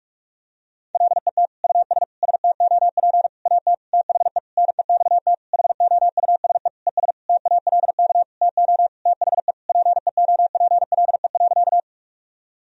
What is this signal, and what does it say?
Morse code, heard as a keyed tone, about 46 wpm, 700 Hz. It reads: LET US STOP AT THE NEXT HOUSE ES TALK TO THE PEOPLE.
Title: Sentences from Top 400 Words and Abbreviations 45wpm